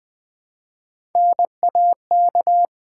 Morse code, heard as a keyed tone, about 20 wpm, 700 Hz. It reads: NAK
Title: Build A Word - 1 Letter At A Time - 5-Letter Words - 20wpm